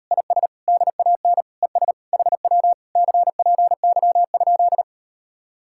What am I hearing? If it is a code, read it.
Morse code, heard as a keyed tone, about 38 wpm, 700 Hz. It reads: IS DAN ES HW CPY?